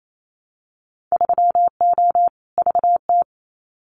Morse code, heard as a keyed tone, about 28 wpm, 700 Hz. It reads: 3O VT
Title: Winter Field Day Exchanges Encore 28wpm